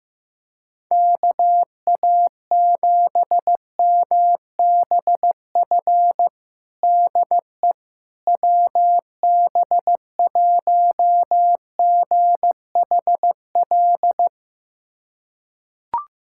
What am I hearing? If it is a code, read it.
Morse code, heard as a keyed tone, about 15 wpm, 700 Hz. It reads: KA7MBF DE WB1GHL
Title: USA Callsign DE Callsign - 6 Characters - 15wpm